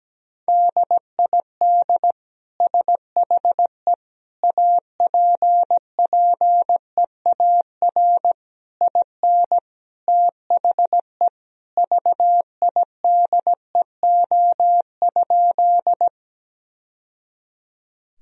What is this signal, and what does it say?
Morse code, heard as a keyed tone, about 17 wpm, 700 Hz. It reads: DID SHE APPEAR IN THE VIDEO?
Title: Sentences from Top 1000 Words - Review Part 4 - 17wpm